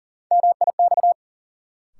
Morse code, heard as a keyed tone, about 40 wpm, 700 Hz. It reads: MIX
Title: Sentences from Top 1000 Words - Review Part 4 - New Word Drill - 40wpm